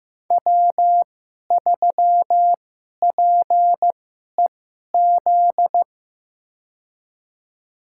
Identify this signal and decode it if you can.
Morse code, heard as a keyed tone, about 15 wpm, spaced as for 11 wpm, 700 Hz. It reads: W3PEZ
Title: USA Call Signs - 5 Characters - 2x Character Spacing - 15wpm